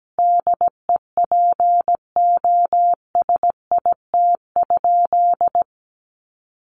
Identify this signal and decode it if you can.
Morse code, heard as a keyed tone, about 17 wpm, 700 Hz. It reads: DEPOSIT?
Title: Sentences from Top 1300 Words 17wpm